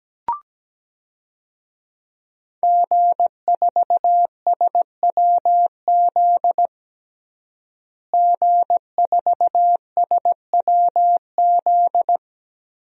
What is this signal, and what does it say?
Morse code, heard as a keyed tone, about 17 wpm, 700 Hz. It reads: G4SWZ G4SWZ
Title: DXCC-100 Call Signs - 5 Characters - Repeated 3x - 17wpm